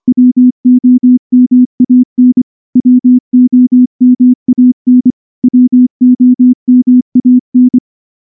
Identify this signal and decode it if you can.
Morse code, heard as a keyed tone, about 24 wpm, 250 Hz. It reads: WOMAN WOMAN WOMAN
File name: morse / morse-training-25WPM